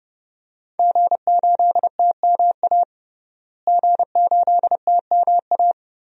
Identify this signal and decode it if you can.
Morse code, heard as a keyed tone, about 30 wpm, 700 Hz. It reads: G8TMA G8TMA
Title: DXCC-100 Call Signs - 5 Characters - Repeated 2x - 30wpm